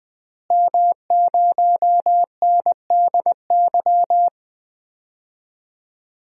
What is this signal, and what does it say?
Morse code, heard as a keyed tone, about 20 wpm, 700 Hz. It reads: M0NDY
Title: DXCC-100 Call Signs - 5 Characters - Repeated 3x - 20wpm